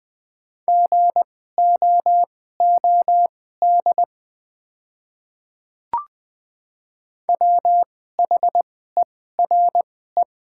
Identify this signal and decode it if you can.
Morse code, heard as a keyed tone, about 20 wpm, spaced as for 15 wpm, 690 Hz. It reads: GOOD WHERE
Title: 2x Character Spacing - Top 200 Words - 20wpm